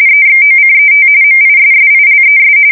RTTY, about 45.45 baud. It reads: EXOCET MISSILES
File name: fsk170 / battleshipnj